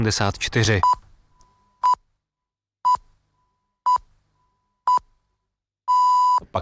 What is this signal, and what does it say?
Radio time signal (time pips), with speech. It is heard from 0.8 to 6.4 s.